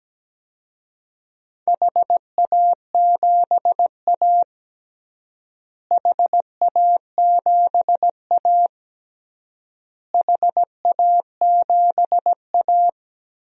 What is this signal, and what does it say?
Morse code, heard as a keyed tone, about 17 wpm, 700 Hz. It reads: HA7A HA7A HA7A
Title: DXCC-100 Call Signs - 4 Characters - Repeated 3x - 17wpm